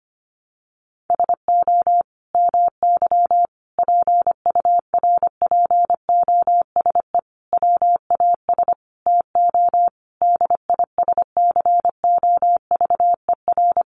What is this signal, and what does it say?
Morse code, heard as a keyed tone, about 25 wpm, 700 Hz. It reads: SO MY PURPOSE WAS TO DISCOVER
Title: Sentences from Top 1100 Words 25wpm